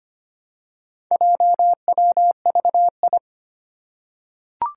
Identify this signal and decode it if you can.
Morse code, heard as a keyed tone, about 25 wpm, 700 Hz. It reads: JWVI
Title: Four Challenging Letter-Number Combinations 25wpm